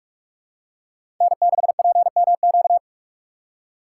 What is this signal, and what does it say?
Morse code, heard as a keyed tone, about 46 wpm, 700 Hz. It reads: N6PMQ
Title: USA Call Signs - 5 Characters - 3x Character Spacing - 45wpm